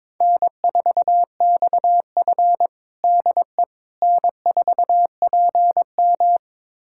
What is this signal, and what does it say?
Morse code, heard as a keyed tone, about 21 wpm, 700 Hz. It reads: N4XF DE N4PM
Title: USA Callsign DE Callsign - 4 Characters - 22wpm